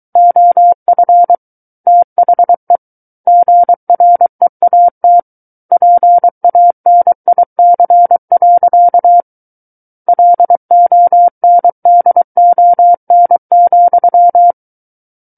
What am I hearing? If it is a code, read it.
Morse code, heard as a keyed tone, about 23 wpm, 700 Hz. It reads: OF THE GREAT PANIC. LONDON,
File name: Book One / 14-Chapter 14 - 23 WPM